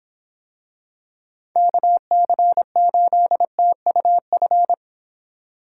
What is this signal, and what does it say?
Morse code, heard as a keyed tone, about 25 wpm, 700 Hz. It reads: KC8TUF